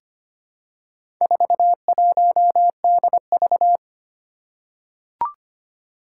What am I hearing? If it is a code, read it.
Morse code, heard as a keyed tone, about 25 wpm, 700 Hz. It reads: 41DV
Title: Four Challenging Letter-Number Combinations 25wpm